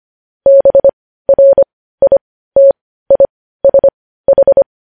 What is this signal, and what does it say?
Morse code, heard as a keyed tone, about 25 wpm, spaced as for 15 wpm, 550 Hz. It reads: BRITISH
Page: news headlines in morse code at 15 wpm